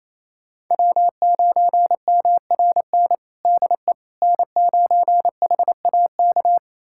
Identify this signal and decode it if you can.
Morse code, heard as a keyed tone, about 28 wpm, 700 Hz. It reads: W9MRN DE N9HAK
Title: USA Callsign DE Callsign - 5 Characters - 28wpm